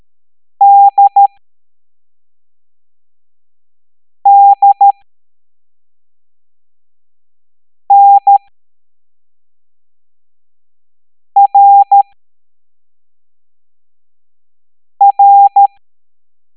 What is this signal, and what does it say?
Morse code, heard as a keyed tone, about 13 wpm, spaced as for 3 wpm, 800 Hz. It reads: DDNRR